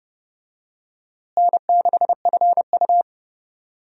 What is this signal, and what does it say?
Morse code, heard as a keyed tone, about 30 wpm, 700 Hz. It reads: N6FU
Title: USA Call Signs - 4 Characters - Repeated 3x - 30wpm